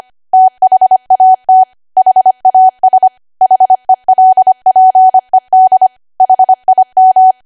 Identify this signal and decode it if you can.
Morse code, heard as a keyed tone, about 25 wpm, 750 Hz. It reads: THAT HAS HELPED HIM
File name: cwa / W1AW-Level3 / 25WPM